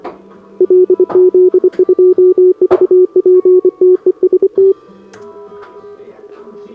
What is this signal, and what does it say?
Morse code, heard as a keyed tone, about 25 wpm, 370 Hz, with radio noise. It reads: LZ2VP TEST
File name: CQWWCW2025 / SobotaOdpol 20m Hudba